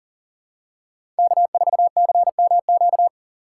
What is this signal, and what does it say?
Morse code, heard as a keyed tone, about 40 wpm, 700 Hz. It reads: K4CMQ
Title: USA Call Signs - 5 Characters - 6x Character Spacing - 40wpm